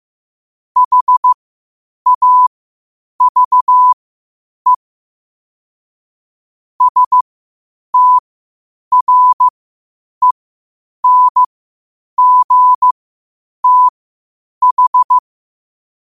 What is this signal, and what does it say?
Morse code, heard as a keyed tone, about 15 wpm, spaced as for 8 wpm, 1000 Hz. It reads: HAVE STRENGTH
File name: qotdfiles / 8wpm